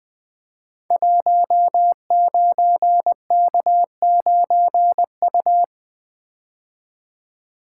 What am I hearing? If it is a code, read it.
Morse code, heard as a keyed tone, about 21 wpm, 700 Hz. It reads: 19K9U
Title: Code Groups - 5 Letter:Numbers - 20wpm